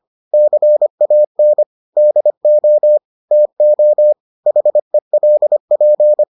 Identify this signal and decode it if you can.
Morse code, heard as a keyed tone, about 25 wpm, 600 Hz. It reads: CAN DO TO HELP